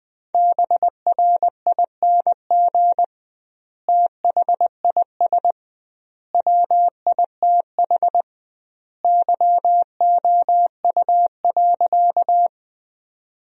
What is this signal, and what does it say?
Morse code, heard as a keyed tone, about 20 wpm, 700 Hz. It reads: BRING THIS WITH YOU.
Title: 2x Word-Spacing - Sentences from Top 500 Words - 20wpm